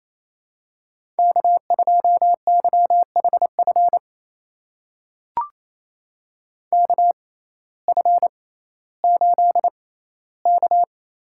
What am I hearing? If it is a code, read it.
Morse code, heard as a keyed tone, about 28 wpm, spaced as for 15 wpm, 700 Hz. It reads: K2YHF KF8K
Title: USA Call Signs - 5 Characters - 6x Character Spacing - 28wpm